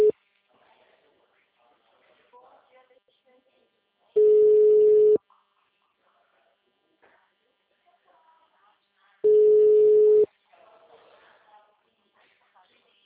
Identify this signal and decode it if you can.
Ringback tone (Europe, 425 Hz), with speech.